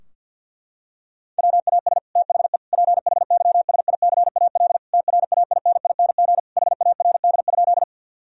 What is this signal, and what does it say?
Morse code, heard as a keyed tone, about 50 wpm, 700 Hz. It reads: WAS THE PHYSICAL TRAINING HARD?
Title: Sentences from Top 1000 Words - Review Part 4 - 50wpm